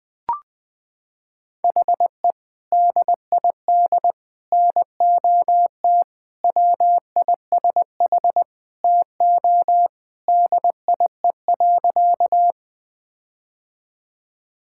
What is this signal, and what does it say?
Morse code, heard as a keyed tone, about 20 wpm, 700 Hz. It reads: HE DID NOT WISH TO DIE.
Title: Sentences from Top 1600 Words 20wpm